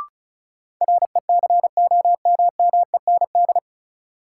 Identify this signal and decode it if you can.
Morse code, heard as a keyed tone, about 36 wpm, 700 Hz. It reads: RECOMMEND
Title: Sentences from Top 1200 Words - New Word Drill Alt - 35wpm